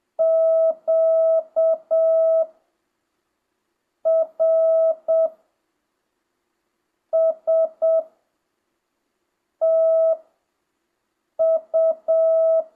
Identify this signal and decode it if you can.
Morse code, heard as a keyed tone, about 7 wpm, spaced as for 4 wpm, 650 Hz. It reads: QRSTU
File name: jh8cbh / 2025cw